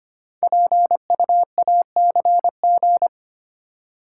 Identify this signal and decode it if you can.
Morse code, heard as a keyed tone, about 25 wpm, 700 Hz. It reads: PUACG
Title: Code Groups - 5 Letters - 25wpm